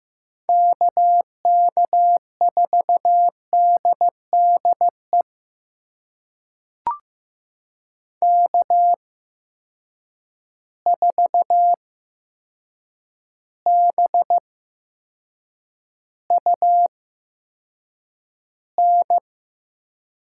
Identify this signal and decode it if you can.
Morse code, heard as a keyed tone, about 15 wpm, spaced as for 7 wpm, 700 Hz. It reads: KK4DDE K4BUN
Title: USA Call Signs - 8x Character Spacing - 15wpm